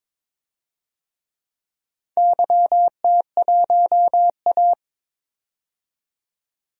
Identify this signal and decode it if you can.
Morse code, heard as a keyed tone, about 22 wpm, 700 Hz. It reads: YT1A
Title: DXCC-100 Call Signs - 4 Characters - 22wpm